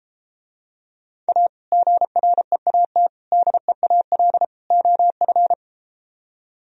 Morse code, heard as a keyed tone, about 33 wpm, 700 Hz. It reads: A GREAT DEAL OF